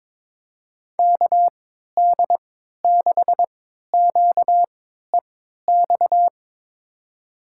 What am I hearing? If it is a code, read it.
Morse code, heard as a keyed tone, about 22 wpm, spaced as for 12 wpm, 700 Hz. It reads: KD6QEX
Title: USA Call Signs - 6 Characters - 3x Character Spacing - 22wpm